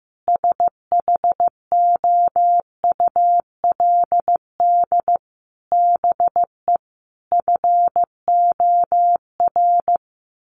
Morse code, heard as a keyed tone, about 14 wpm, 700 Hz. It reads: SHOULD BE FOR
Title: Sentences from Top 1300 Words 15wpm